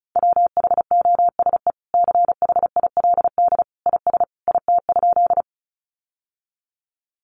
Morse code, heard as a keyed tone, about 34 wpm, 700 Hz. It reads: WHOSE CHILD IS IT?
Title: Sentences from Top 1200 Words 35wpm